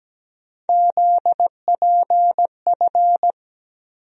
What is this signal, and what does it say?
Morse code, heard as a keyed tone, about 17 wpm, 700 Hz. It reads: ZPF
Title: USA Call Sign Postfixes - 3 Letters 17wpm